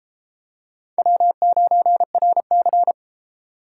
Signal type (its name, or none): Morse code